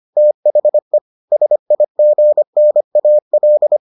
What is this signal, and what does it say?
Morse code, heard as a keyed tone, about 25 wpm, 600 Hz. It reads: THE SIGNAL